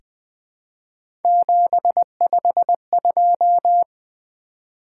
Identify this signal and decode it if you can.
Morse code, heard as a keyed tone, about 20 wpm, 700 Hz. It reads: 752
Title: Three Letter-Number Combinations - 20wpm